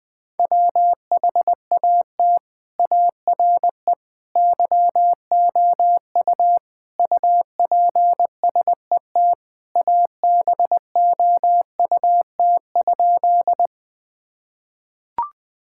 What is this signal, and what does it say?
Morse code, heard as a keyed tone, about 21 wpm, 700 Hz. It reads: WHAT ARE YOU UPSET ABOUT?
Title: Sentences from Top 1000 Words - Review Part 5 - 20wpm